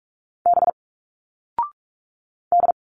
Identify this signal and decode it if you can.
Morse code, heard as a keyed tone, about 46 wpm, 700 Hz. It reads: B D
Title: Single Letters 45wpm